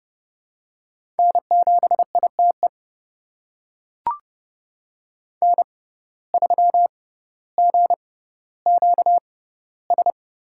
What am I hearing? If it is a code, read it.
Morse code, heard as a keyed tone, about 30 wpm, spaced as for 15 wpm, 700 Hz. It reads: N7ITE N3GQS
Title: USA Call Signs - 5 Characters - 6x Character Spacing - 30wpm